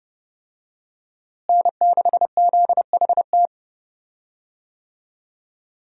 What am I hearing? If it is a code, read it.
Morse code, heard as a keyed tone, about 30 wpm, 700 Hz. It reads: N6ZHT
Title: USA Call Signs - 5 Characters - Repeated 3x - 30wpm